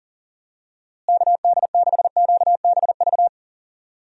Morse code, heard as a keyed tone, about 40 wpm, 700 Hz. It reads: KD6QBV
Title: USA Call Signs - 6 Characters - Repeated 2x - 40wpm